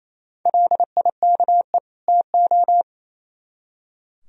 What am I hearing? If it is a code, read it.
Morse code, heard as a keyed tone, about 35 wpm, 700 Hz. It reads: LIKE TO
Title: Speed-Racing - Sets of 2 Words 28wpm